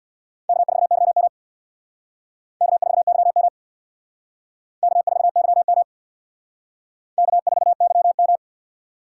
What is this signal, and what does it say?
Morse code, heard as a keyed tone, about 61 wpm, 700 Hz. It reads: K4YM K4YM K4YM K4YM
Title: Speed Racing - USA 1x2 Call Signs 50wpm